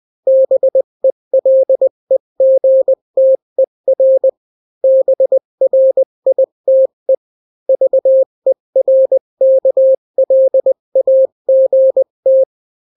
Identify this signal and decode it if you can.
Morse code, heard as a keyed tone, about 20 wpm, 530 Hz. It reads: BELEGTER BRITE VERKLAGT
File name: daily / 18wpm